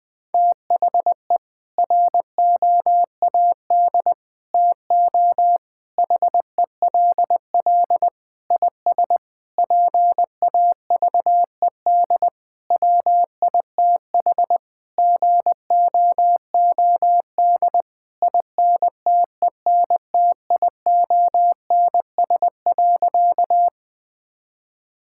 Morse code, heard as a keyed tone, about 20 wpm, 700 Hz. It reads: THE ROAD TO HELL IS PAVED WITH GOOD INTENTIONS.
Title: Sentences with 8-Word Phrases - Spoken Beginning - 20wpm